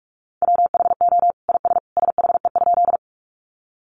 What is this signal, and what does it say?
Morse code, heard as a keyed tone, about 44 wpm, 700 Hz. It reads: WHO IS SHE?
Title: Sentences from Top 100 Words 45wpm